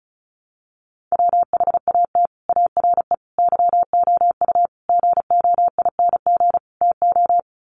Morse code, heard as a keyed tone, about 34 wpm, 700 Hz. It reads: WHAT ARE YOU GOING TO
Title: Sets of 5 Words 35wpm